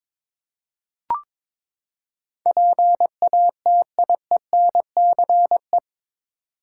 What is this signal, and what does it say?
Morse code, heard as a keyed tone, about 22 wpm, 700 Hz. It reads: PATIENCE